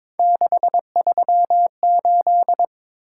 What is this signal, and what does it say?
Morse code, heard as a keyed tone, about 22 wpm, 700 Hz. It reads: 638
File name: practice / 22wpm / Three Letter-Number Combinations - 22wpm